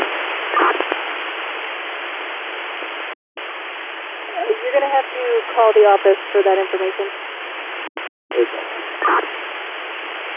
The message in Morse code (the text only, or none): none